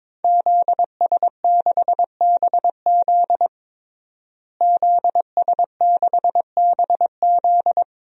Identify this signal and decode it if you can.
Morse code, heard as a keyed tone, about 22 wpm, 700 Hz. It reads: ZS6BZ ZS6BZ
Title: DXCC-100 Call Signs - 5 Characters - Repeated 2x - 22wpm